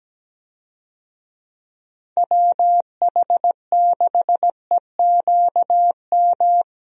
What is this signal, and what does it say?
Morse code, heard as a keyed tone, about 17 wpm, 700 Hz. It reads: WH6EQM
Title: USA Call Signs - 6 Characters - Repeated 2x - 17wpm